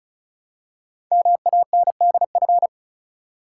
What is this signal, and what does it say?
Morse code, heard as a keyed tone, about 36 wpm, 690 Hz. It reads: MANDF
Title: Code Groups - 5 Letters - 35wpm